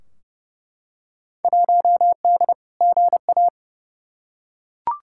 Morse code, heard as a keyed tone, about 30 wpm, 700 Hz. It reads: 1D GA
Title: Field Day Exchanges - 30wpm